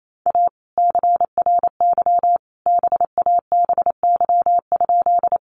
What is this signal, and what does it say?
Morse code, heard as a keyed tone, about 28 wpm, 700 Hz. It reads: A CRY BABY?
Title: Sentences from Top 1300 Words 28wpm